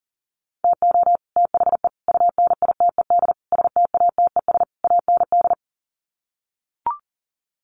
Morse code, heard as a keyed tone, about 40 wpm, 700 Hz. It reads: TO THE UNITED STATES AND
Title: Sets of 5 Words Encore 40wpm